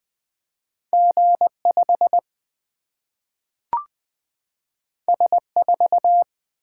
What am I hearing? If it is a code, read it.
Morse code, heard as a keyed tone, about 20 wpm, 700 Hz. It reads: G 5 S 4
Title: Single-Letter Single-Number 20wpm